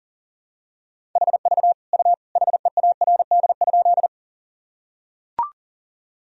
Morse code, heard as a keyed tone, about 40 wpm, 700 Hz. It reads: HV U HEARD?
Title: Sentences from Top 400 Words and Abbreviations 40wpm